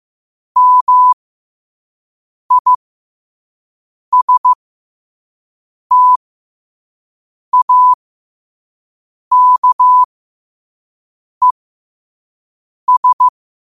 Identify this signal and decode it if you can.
Morse code, heard as a keyed tone, about 15 wpm, spaced as for 5 wpm, 1000 Hz. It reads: MISTAKES
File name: qotdfiles / 5wpm